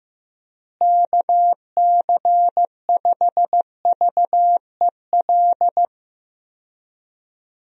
Morse code, heard as a keyed tone, about 15 wpm, 700 Hz. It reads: KC5VEL
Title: USA Call Signs - 6 Characters - Repeated 3x - 15wpm